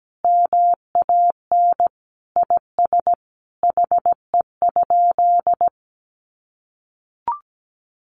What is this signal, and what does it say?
Morse code, heard as a keyed tone, about 17 wpm, 700 Hz. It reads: MAN IS HE?
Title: Sentences from Top 1300 Words 17wpm